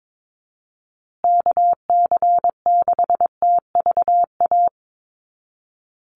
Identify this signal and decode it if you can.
Morse code, heard as a keyed tone, about 22 wpm, 700 Hz. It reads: KC6TVA
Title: USA Call Signs Repeated 3x 22wpm